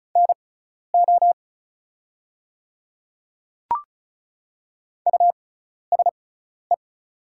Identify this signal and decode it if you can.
Morse code, heard as a keyed tone, about 34 wpm, spaced as for 12 wpm, 700 Hz. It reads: NO USE
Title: 6x Character Spacing - Top 100 Words - 35wpm